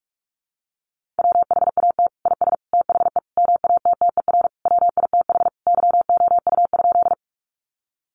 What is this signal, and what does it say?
Morse code, heard as a keyed tone, about 44 wpm, 700 Hz. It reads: WHAT IS THE MATTER WITH YOU?
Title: Sentences from Top 700 Words 45wpm